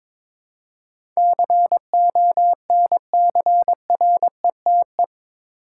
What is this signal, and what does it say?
Morse code, heard as a keyed tone, about 22 wpm, 700 Hz. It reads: CONCRETE
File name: practice / 22wpm / Eight Letter Words - 22wpm